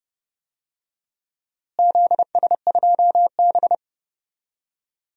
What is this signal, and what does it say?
Morse code, heard as a keyed tone, about 30 wpm, 700 Hz. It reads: ZS2B
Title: DXCC-100 Call Signs - 4 Characters - Repeated 2x - 30wpm